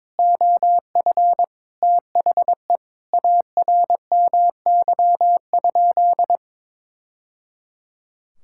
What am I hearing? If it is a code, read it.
Morse code, heard as a keyed tone, about 21 wpm, 700 Hz. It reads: OF THE ARMY?
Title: Sentences from Top 1000 Words - Review Part 6 - 22wpm